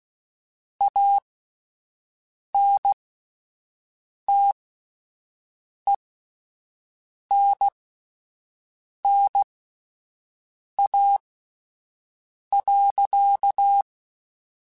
Morse code, heard as a keyed tone, about 16 wpm, spaced as for 5 wpm, 800 Hz. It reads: ANTENNA.